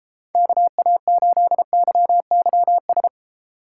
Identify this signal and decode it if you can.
Morse code, heard as a keyed tone, about 32 wpm, 700 Hz. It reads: KA8YYS